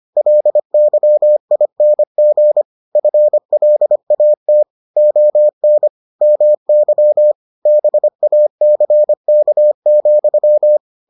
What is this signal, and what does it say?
Morse code, heard as a keyed tone, about 25 wpm, 600 Hz. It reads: LYING FLAT ON MY BACK,